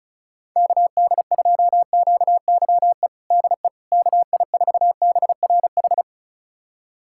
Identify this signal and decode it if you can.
Morse code, heard as a keyed tone, about 34 wpm, 700 Hz. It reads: KD2QYE DE KI4BRH
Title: USA Callsign DE Callsign 35wpm